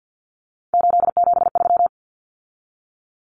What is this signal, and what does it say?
Morse code, heard as a keyed tone, about 50 wpm, 700 Hz. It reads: 873
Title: Common Cut Numbers - 3-Digit 50wpm